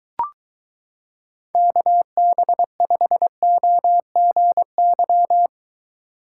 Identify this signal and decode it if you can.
Morse code, heard as a keyed tone, about 22 wpm, 700 Hz. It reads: KB5OGY